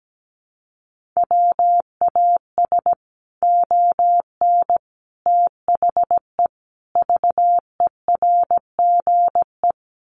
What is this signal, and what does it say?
Morse code, heard as a keyed tone, about 17 wpm, 700 Hz. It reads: WAS ON THE VERGE
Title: Sets of 5 Words Encore 17wpm